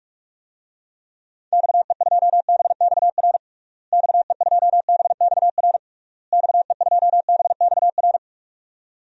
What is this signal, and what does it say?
Morse code, heard as a keyed tone, about 46 wpm, 700 Hz. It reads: XE2BXR XE2BXR XE2BXR
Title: DXCC-100 Call Signs - 6 Characters - Repeated 3x - 45wpm